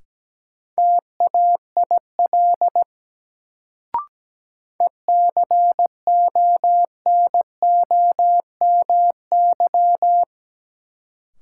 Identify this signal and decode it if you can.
Morse code, heard as a keyed tone, about 17 wpm, 700 Hz. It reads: TAIL ECONOMY
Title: Sentences from Top 1000 Words - Review Part 3 - New Word Drill - 17wpm